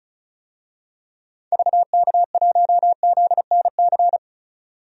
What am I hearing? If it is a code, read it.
Morse code, heard as a keyed tone, about 36 wpm, 700 Hz. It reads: VK1ZNC
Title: DXCC-100 Call Signs - 6 Characters - 35wpm